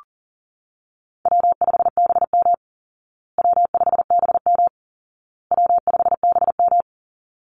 Morse code, heard as a keyed tone, about 40 wpm, 700 Hz. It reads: W5BM W5BM W5BM
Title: Easy Call Signs Repeated 3x - 40wpm